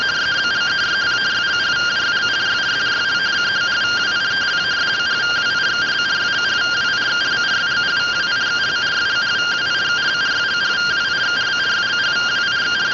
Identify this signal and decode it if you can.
Two-tone teleprinter signal, about 170 Hz shift.